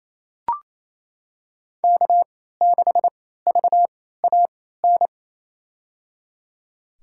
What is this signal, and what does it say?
Morse code, heard as a keyed tone, about 28 wpm, spaced as for 16 wpm, 700 Hz. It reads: K6VAN
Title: USA Call Signs - 3x Character Spacing - 28wpm